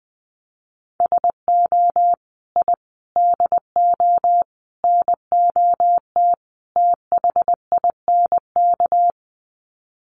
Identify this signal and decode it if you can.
Morse code, heard as a keyed tone, about 20 wpm, 700 Hz. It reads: SO I DO NOT THINK
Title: Sets of 5 Words 20wpm